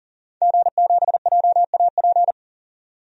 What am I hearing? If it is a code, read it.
Morse code, heard as a keyed tone, about 40 wpm, 700 Hz. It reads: G7JAP
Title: DXCC-100 Call Signs - 5 Characters - 40wpm